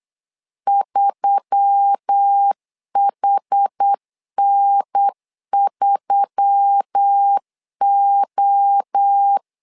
Morse code, heard as a keyed tone, about 8 wpm, 800 Hz. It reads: 3HN3O
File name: LEZIONI / 4 Q1S3 8wpm 3037